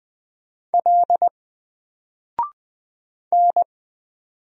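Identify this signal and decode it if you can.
Morse code, heard as a keyed tone, about 20 wpm, 700 Hz. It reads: L N
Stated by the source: Single letter and number Morse drill at 20 wpm, from N4LCD.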